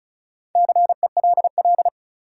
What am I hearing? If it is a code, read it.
Morse code, heard as a keyed tone, about 36 wpm, 700 Hz. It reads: CELL
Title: Sentences from Top 1100 Words - New Word Drill Alt - 35wpm